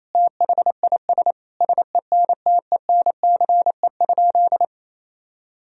Morse code, heard as a keyed tone, about 28 wpm, 700 Hz. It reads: THIS SENTENCE?
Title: Sentences from Top 1000 Words - Review Part 1 - 28wpm